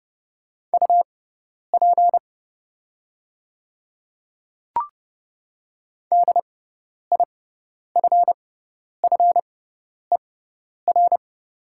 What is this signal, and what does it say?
Morse code, heard as a keyed tone, about 30 wpm, spaced as for 10 wpm, 700 Hz. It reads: UP DIFFER
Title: 6x Character Spacing - Top 300 Words - 30wpm